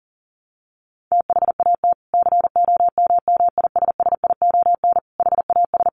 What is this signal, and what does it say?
Morse code, heard as a keyed tone, about 40 wpm, 700 Hz. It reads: THAT COMMISSION HAS
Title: Sentences from Top 1400 Words 40wpm